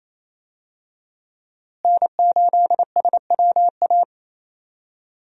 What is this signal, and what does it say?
Morse code, heard as a keyed tone, about 27 wpm, 700 Hz. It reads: N8SWA
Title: USA Call Signs - 5 Characters - Repeated 2x - 28wpm